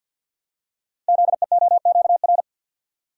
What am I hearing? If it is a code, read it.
Morse code, heard as a keyed tone, about 48 wpm, 700 Hz. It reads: ZEOQR
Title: Code Groups - 5 Letters - 50wpm